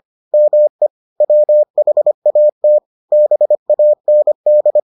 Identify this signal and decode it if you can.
Morse code, heard as a keyed tone, about 25 wpm, 600 Hz. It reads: ME WHAT BAND